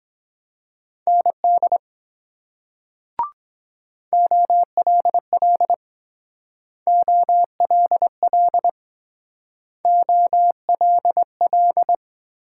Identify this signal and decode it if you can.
Morse code, heard as a keyed tone, about 22 wpm, 700 Hz. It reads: ND OLL OLL OLL